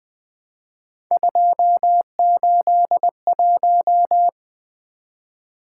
Morse code, heard as a keyed tone, about 21 wpm, 700 Hz. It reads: 281
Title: Three Letter-Number Combinations - 20wpm